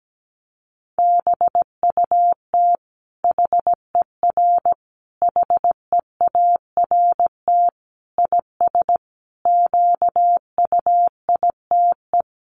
Morse code, heard as a keyed tone, about 17 wpm, 700 Hz. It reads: BUT HER HEART IS QUITE
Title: Sentences from Top 1200 Words 17wpm